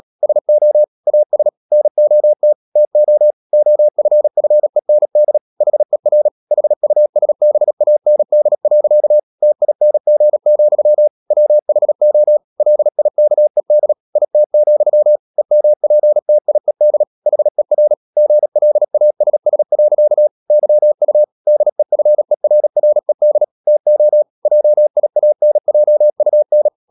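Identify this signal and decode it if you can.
Morse code, heard as a keyed tone, about 37 wpm, 600 Hz. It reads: SO AS NOT TO OFFEND HER HUSBAND. TING, WHO LIKED IT, EMPTIED HER GLASS. YU DEFERRED TO JIANJUN
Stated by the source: Morse code practice at 35–39 wpm sending English text of Ken Follett's novel Never, from CW med Gnister.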